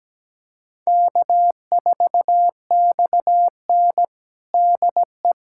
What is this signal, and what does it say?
Morse code, heard as a keyed tone, about 17 wpm, 700 Hz. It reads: K4XN DE
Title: USA Callsign DE Callsign - 4 Characters - 17wpm